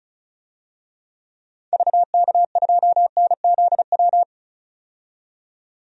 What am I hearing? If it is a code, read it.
Morse code, heard as a keyed tone, about 36 wpm, 700 Hz. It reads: VK2NZW
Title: DXCC-100 Call Signs - 6 Characters - 35wpm